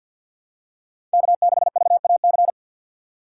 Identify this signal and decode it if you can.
Morse code, heard as a keyed tone, about 50 wpm, 700 Hz. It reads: K6VAC
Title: USA Call Signs - 5 Characters - 4x Character Spacing - 50wpm